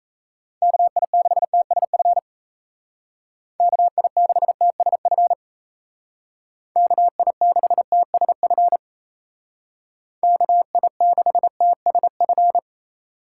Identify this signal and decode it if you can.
Morse code, heard as a keyed tone, about 34 wpm, 700 Hz. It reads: KI6TSF KI6TSF KI6TSF KI6TSF
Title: Speed Racing - USA 2x3 Call Signs 28wpm